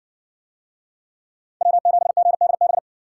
Morse code, heard as a keyed tone, about 57 wpm, 700 Hz. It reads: W7GDB